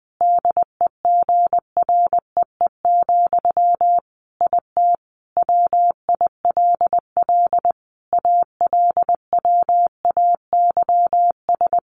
Morse code, heard as a keyed tone, about 20 wpm, 700 Hz. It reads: DEGREE, IT WILL ALWAYS
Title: Sentences from Top 1500 Words 20wpm